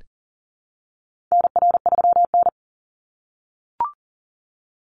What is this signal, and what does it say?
Morse code, heard as a keyed tone, about 41 wpm, 700 Hz. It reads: NR3N
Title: Call Signs - 40wpm